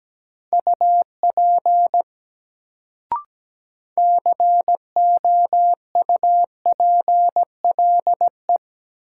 Morse code, heard as a keyed tone, about 17 wpm, 700 Hz. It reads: UP COUPLE
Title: Sets of 2 Words - New Word Drill - 17wpm